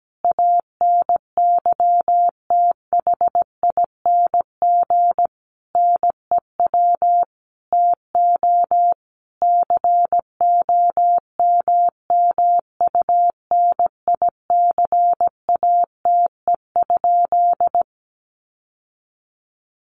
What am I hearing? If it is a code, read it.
Morse code, heard as a keyed tone, about 17 wpm, 700 Hz. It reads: ANYTHING NEW TO COMMUNICATE?
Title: Sentences from Top 1300 Words 17wpm